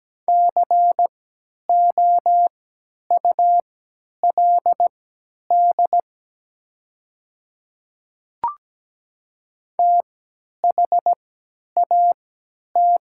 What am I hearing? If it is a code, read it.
Morse code, heard as a keyed tone, about 17 wpm, spaced as for 10 wpm, 700 Hz. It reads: COULD THAT
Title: 3x Character Spacing - Top 100 Words - 17wpm